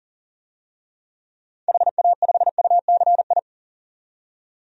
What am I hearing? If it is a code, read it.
Morse code, heard as a keyed tone, about 40 wpm, 700 Hz. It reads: HA5UCI
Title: DXCC-100 Call Signs - 6 Characters - 40wpm